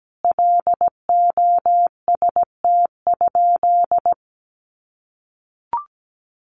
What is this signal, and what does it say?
Morse code, heard as a keyed tone, about 17 wpm, 700 Hz. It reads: LOST?
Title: Sentences from Top 1400 Words 17wpm